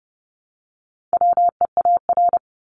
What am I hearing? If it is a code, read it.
Morse code, heard as a keyed tone, about 30 wpm, 700 Hz. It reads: WEAR
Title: Top 1251-1500 Words 30wpm